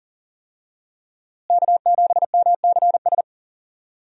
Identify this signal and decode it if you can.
Morse code, heard as a keyed tone, about 40 wpm, 700 Hz. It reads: K7MCS
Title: USA Call Signs - 5 Characters - Repeated 3x - 40wpm